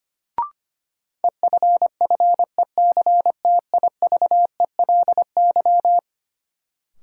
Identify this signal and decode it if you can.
Morse code, heard as a keyed tone, about 25 wpm, 700 Hz. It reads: EFFECTIVELY
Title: Sentences from Top 1400 Words - New Word Drill - 25wpm